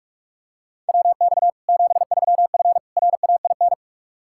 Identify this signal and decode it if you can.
Morse code, heard as a keyed tone, about 46 wpm, 700 Hz. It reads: WX 73F RAIN
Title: Practice QSO - Exchange 2 - Info - 45wpm